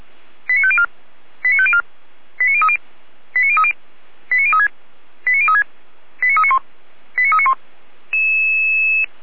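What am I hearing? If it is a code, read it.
Five-tone selective call (ZVEI), heard as quick runs of five tones, tones of about 70 ms. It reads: 89504 89504 89030 89030 89036 89036 89391 89391
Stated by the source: ZVEI five-tone selective call received via SDR.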